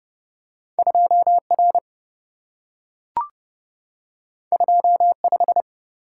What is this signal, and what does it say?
Morse code, heard as a keyed tone, about 27 wpm, 700 Hz. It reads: 2R 25